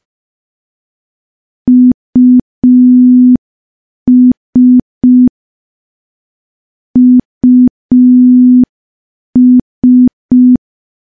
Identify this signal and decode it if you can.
Morse code, heard as a keyed tone, about 5 wpm, 250 Hz. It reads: US US
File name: morse / morse-training-05WPM